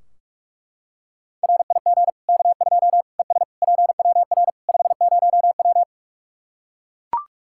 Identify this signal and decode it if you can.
Morse code, heard as a keyed tone, about 42 wpm, 700 Hz. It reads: RIG K2 ES PWR 50W
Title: Practice QSO - Exchange 2 - Info - 45wpm